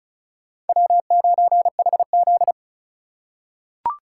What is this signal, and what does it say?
Morse code, heard as a keyed tone, about 35 wpm, 700 Hz. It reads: W9HZ
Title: USA Call Signs - 4 Characters - 2x Character Spacing - 35wpm